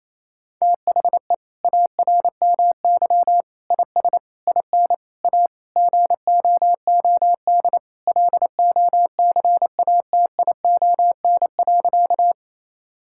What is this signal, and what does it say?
Morse code, heard as a keyed tone, about 27 wpm, 690 Hz. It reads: THE ARMY IS IN A GOOD LOCATION.
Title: Sentences from Top 1000 Words - Review Part 6 - 28wpm